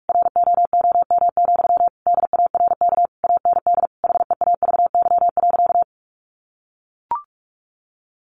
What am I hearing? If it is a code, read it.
Morse code, heard as a keyed tone, about 44 wpm, 700 Hz. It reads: ROOM, DARK AND HEAVY.
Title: Sentences from Top 1400 Words 45wpm